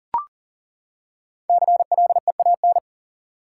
Morse code, heard as a keyed tone, about 39 wpm, 700 Hz. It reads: CLEAN
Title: Five Letter Words - 40wpm